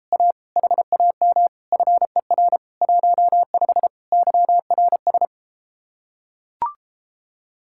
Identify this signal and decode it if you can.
Morse code, heard as a keyed tone, about 33 wpm, 700 Hz. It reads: A HAM FER 15 YRS